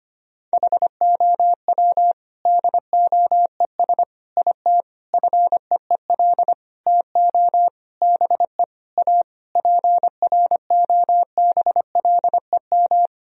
Morse code, heard as a keyed tone, about 25 wpm, 700 Hz. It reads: HOW DOES IT FEEL TO BE A PROBLEM
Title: Sentences from Top 1000 Words - Review Part 2 - 25wpm